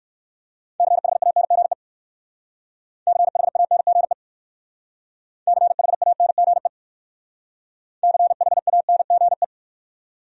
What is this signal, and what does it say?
Morse code, heard as a keyed tone, about 60 wpm, 690 Hz. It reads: CHANGE CHANGE CHANGE CHANGE